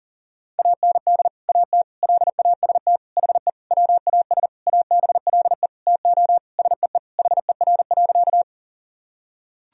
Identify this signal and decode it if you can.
Morse code, heard as a keyed tone, about 40 wpm, 700 Hz. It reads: AND AT LAST HE WAS ABLE TO SEE HER.
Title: Sentences from Top 1000 Words - Review Part 2 - 40wpm